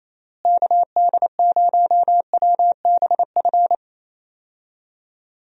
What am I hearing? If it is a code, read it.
Morse code, heard as a keyed tone, about 28 wpm, 700 Hz. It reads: KD0WBF
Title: USA Call Signs - 6 Characters - Repeated 3x - 28wpm